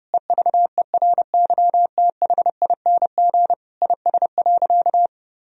Morse code, heard as a keyed tone, about 32 wpm, 700 Hz. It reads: EVERYTHING IS.
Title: Sentences from Top 1000 Words - Review Part 4 - 30wpm